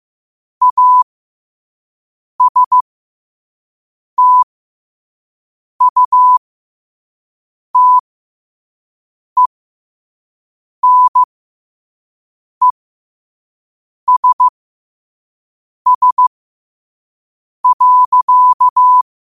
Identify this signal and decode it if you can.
Morse code, heard as a keyed tone, about 15 wpm, spaced as for 5 wpm, 1000 Hz. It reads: ASTUTENESS.
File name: qotdfiles / 5wpm